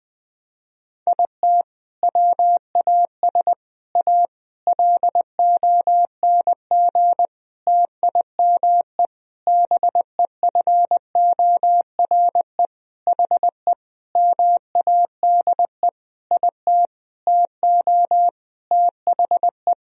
Morse code, heard as a keyed tone, about 20 wpm, 700 Hz. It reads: IT WAS A LONG TIME BEFORE HE MADE IT TO THE
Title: Sentences from Top 1000 Words - Review Part 2 - 20wpm